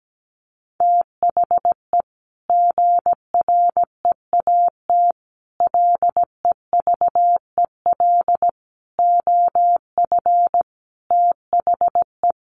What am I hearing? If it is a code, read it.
Morse code, heard as a keyed tone, about 17 wpm, 700 Hz. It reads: THE GREAT LEVEL OF THE
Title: A Sound Of Thunder 17wpm